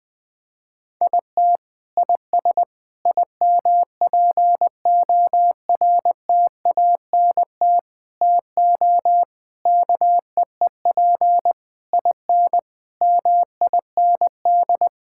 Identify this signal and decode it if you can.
Morse code, heard as a keyed tone, about 20 wpm, 700 Hz. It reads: IT IS IMPORTANT TO KEEP IN MIND
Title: Sentences with 8-Word Phrases - 20wpm